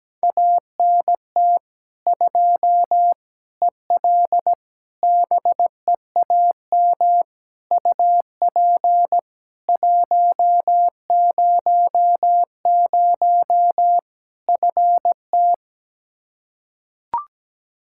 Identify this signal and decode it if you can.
Morse code, heard as a keyed tone, about 17 wpm, 700 Hz. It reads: ANT 2 EL BEAM UP 100 FT
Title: K1USN SST QSO Practice 17wpm